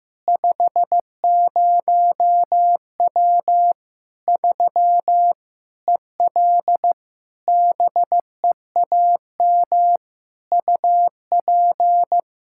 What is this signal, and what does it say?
Morse code, heard as a keyed tone, about 16 wpm, 700 Hz. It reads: 50W 3 EL BEAM UP
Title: Practice QSO - Exchange 2 - Info - 15wpm